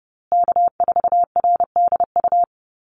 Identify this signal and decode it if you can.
Morse code, heard as a keyed tone, about 30 wpm, 700 Hz. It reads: K4RDU
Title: USA Call Signs Repeated 3x 30wpm